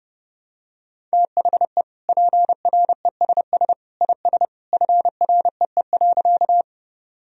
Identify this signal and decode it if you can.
Morse code, heard as a keyed tone, about 32 wpm, 700 Hz. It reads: THE PRESS IS FREE.
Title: Sentences from Top 1000 Words - Review Part 1 - 30wpm